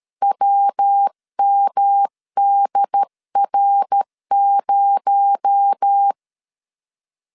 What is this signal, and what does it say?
Morse code, heard as a keyed tone, about 13 wpm, 800 Hz. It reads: WMDR0
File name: LEZIONI / 7 GF9Z / 7 G 12wpm 2055